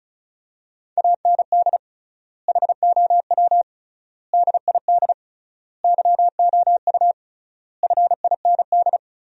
Morse code, heard as a keyed tone, about 36 wpm, 700 Hz. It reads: AND HOW DID YOU FIND
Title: 3x Word-Spacing - Sentences from Top 100 Words - 35wpm